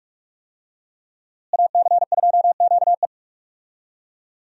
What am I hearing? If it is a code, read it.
Morse code, heard as a keyed tone, about 46 wpm, 700 Hz. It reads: AC2QE
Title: USA Call Signs - 5 Characters - Repeated 2x - 45wpm